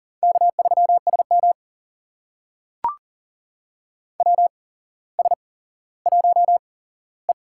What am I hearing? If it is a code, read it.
Morse code, heard as a keyed tone, about 41 wpm, spaced as for 17 wpm, 690 Hz. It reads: K3SM WS1E
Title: USA Call Signs - 4 Characters - 8x Character Spacing - 40wpm